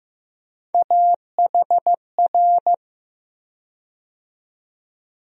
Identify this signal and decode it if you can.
Morse code, heard as a keyed tone, about 15 wpm, 700 Hz. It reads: AHR
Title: USA Call Sign Postfixes - 3 Letters 15wpm